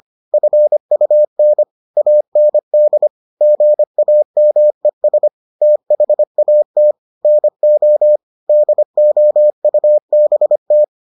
Morse code, heard as a keyed tone, about 25 wpm, 600 Hz. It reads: FUN AND GAMES THAT NO DOUBT